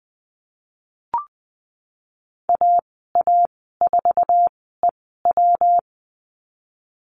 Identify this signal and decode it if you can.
Morse code, heard as a keyed tone, about 20 wpm, spaced as for 15 wpm, 700 Hz. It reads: AA4EW
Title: 140 Lesson 23g - Letter f - Call Signs 20wpm